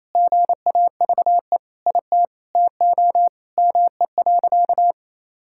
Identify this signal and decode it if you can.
Morse code, heard as a keyed tone, about 28 wpm, 700 Hz. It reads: GAVE IT TO ME.
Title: Sentences from Top 1600 Words 28wpm